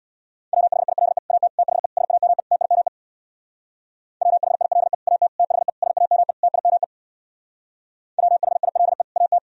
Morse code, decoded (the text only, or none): WHILE AT THE SAME TIME WHILE AT THE SAME TIME WHILE AT